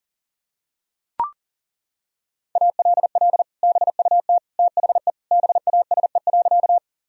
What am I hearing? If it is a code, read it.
Morse code, heard as a keyed tone, about 40 wpm, 700 Hz. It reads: ALL BUT THE BASE.
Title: Sentences from Top 1000 Words - Review Part 1 - 40wpm